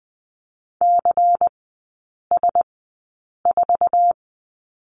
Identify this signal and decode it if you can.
Morse code, heard as a keyed tone, about 20 wpm, spaced as for 8 wpm, 690 Hz. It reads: CS4
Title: 191 Lesson 31e - Letter v - Three Characters 20wpm